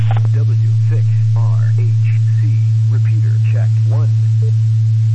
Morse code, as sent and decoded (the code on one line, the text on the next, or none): none